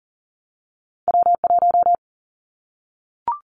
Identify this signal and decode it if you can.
Morse code, heard as a keyed tone, about 40 wpm, 700 Hz. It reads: W1
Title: Callsign Prefixes for Top 25 DXCC - Repeated 3x - 40wpm